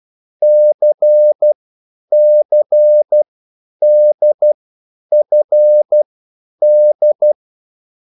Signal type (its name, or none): Morse code